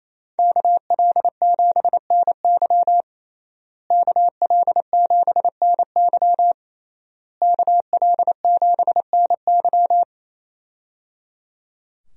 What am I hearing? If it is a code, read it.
Morse code, heard as a keyed tone, about 27 wpm, 700 Hz. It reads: KL7NY KL7NY KL7NY
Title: USA Call Signs - 5 Characters - Repeated 3x - 28wpm